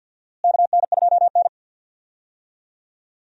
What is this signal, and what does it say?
Morse code, heard as a keyed tone, about 46 wpm, 700 Hz. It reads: KN2N
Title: USA Call Signs - 4 Characters - Repeated 3x - 50wpm